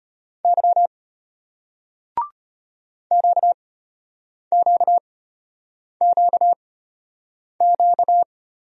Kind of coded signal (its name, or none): Morse code